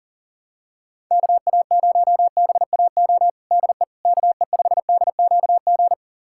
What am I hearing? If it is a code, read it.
Morse code, heard as a keyed tone, about 37 wpm, 700 Hz. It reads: KA0BAO DE KE5DQG
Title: USA Callsign DE Callsign 40wpm